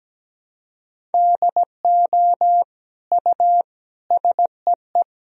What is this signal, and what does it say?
Morse code, heard as a keyed tone, about 17 wpm, 700 Hz. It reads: DO U SEE
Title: Sentences from Top 100 Words and Abbreviations 17wpm